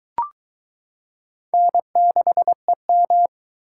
Morse code, heard as a keyed tone, about 22 wpm, 700 Hz. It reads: N6EM